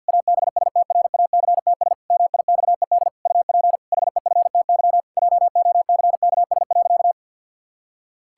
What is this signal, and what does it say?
Morse code, heard as a keyed tone, about 49 wpm, 700 Hz. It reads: ABSTRACTS MIXED UP HEFTY JOCKS.
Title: Pangram Challenge v2 - 50wpm